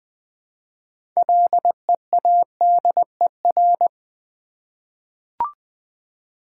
Morse code, heard as a keyed tone, about 20 wpm, 700 Hz. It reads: LEADER
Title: 6 Letter Words - First 3 Letters Given - 20wpm